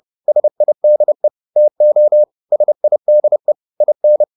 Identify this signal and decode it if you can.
Morse code, heard as a keyed tone, about 30 wpm, 600 Hz. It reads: SIDE TO SIDE IN
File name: the old man and the sea 2 30wpm